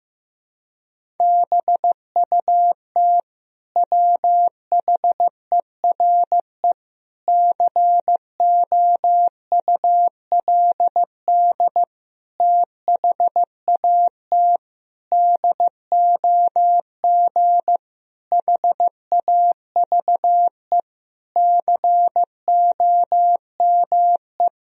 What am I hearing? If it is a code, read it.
Morse code, heard as a keyed tone, about 15 wpm, 700 Hz. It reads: BUT WHERE COULD THAT DOG HAVE COME
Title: Sentences from Top 1000 Words - Review Part 2 - 15wpm